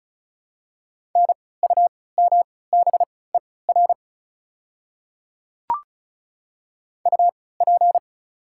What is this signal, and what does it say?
Morse code, heard as a keyed tone, about 36 wpm, spaced as for 20 wpm, 700 Hz. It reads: NUMBER UP
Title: 3x Character Spacing - Top 100 Words - 35wpm